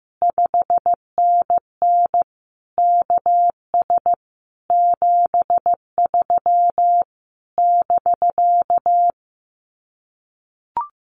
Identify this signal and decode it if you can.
Morse code, heard as a keyed tone, about 15 wpm, 700 Hz. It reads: 5NN KS 73 <BK>
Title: Common QSO Phrases Encore 15wpm